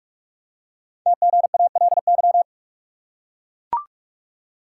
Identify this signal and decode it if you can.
Morse code, heard as a keyed tone, about 44 wpm, 700 Hz. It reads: TGALY
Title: Code Groups - 5 Letters - 45wpm